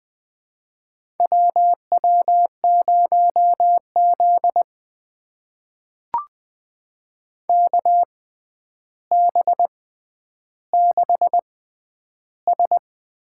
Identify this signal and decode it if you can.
Morse code, heard as a keyed tone, about 20 wpm, spaced as for 10 wpm, 700 Hz. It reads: WW0Z KB6S
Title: USA Call Signs - 6x Character Spacing - 20wpm